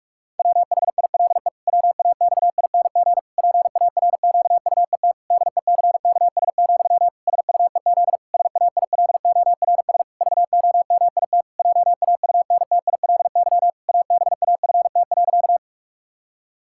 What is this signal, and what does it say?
Morse code, heard as a keyed tone, about 46 wpm, 700 Hz. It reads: WHILE WAXING PARQUET DECKS, SUEZ SAILORS VOMIT JAUNTILY ABAFT.
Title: Pangram Challenge v2 - 45wpm